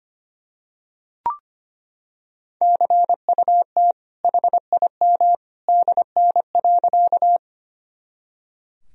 Morse code, heard as a keyed tone, about 25 wpm, 700 Hz. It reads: CUT HIM DN.
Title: Sentences from Top 300 Words and Abbreviations 25wpm